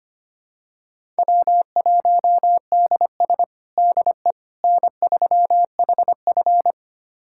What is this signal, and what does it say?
Morse code, heard as a keyed tone, about 25 wpm, 700 Hz. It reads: W1DS DE N3HF
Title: USA Callsign DE Callsign - 4 Characters - 25wpm